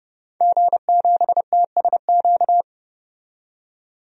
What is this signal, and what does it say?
Morse code, heard as a keyed tone, about 30 wpm, 700 Hz. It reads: G7TSQ
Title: DXCC-100 Call Signs - 5 Characters - 30wpm